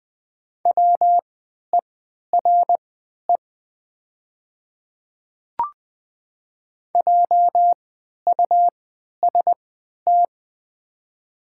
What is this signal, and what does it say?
Morse code, heard as a keyed tone, about 20 wpm, spaced as for 11 wpm, 700 Hz. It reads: WERE JUST
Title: 3x Character Spacing - Top 200 Words - 20wpm